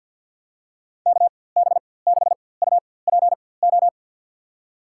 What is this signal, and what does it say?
Morse code, heard as a keyed tone, about 47 wpm, spaced as for 23 wpm, 700 Hz. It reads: KB6UPO
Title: USA Call Signs - 6 Characters - 4x Character Spacing - 50wpm